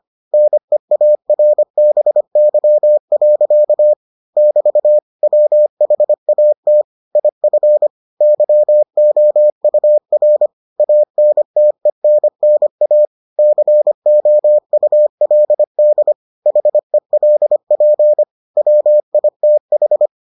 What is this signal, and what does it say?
Morse code, heard as a keyed tone, about 25 wpm, 600 Hz. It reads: NEARBY. <BT> WHAT IF YOUR ANTENNA COULD HELP WITH